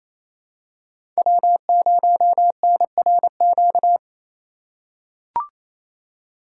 Morse code, heard as a keyed tone, about 28 wpm, 700 Hz. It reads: W0NRQ
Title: USA Call Signs - 5 Characters - 2x Character Spacing - 28wpm